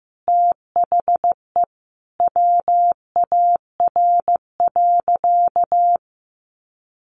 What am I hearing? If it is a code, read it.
Morse code, heard as a keyed tone, about 15 wpm, 700 Hz. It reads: THE WAR.
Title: Sentences from Top 1500 Words 15wpm